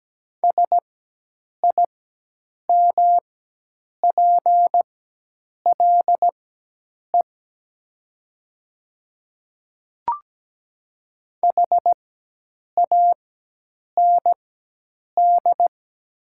Morse code, heard as a keyed tone, about 17 wpm, spaced as for 8 wpm, 700 Hz. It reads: SIMPLE HAND
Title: 4x Character Spacing - Top 500 Words - 17wpm